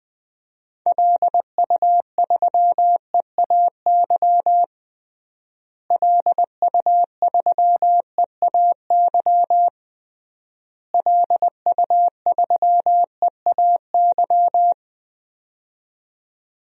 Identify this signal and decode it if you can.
Morse code, heard as a keyed tone, about 20 wpm, 700 Hz. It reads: LU3EAY LU3EAY LU3EAY
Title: DXCC-100 Call Signs - 6 Characters - Repeated 3x - 20wpm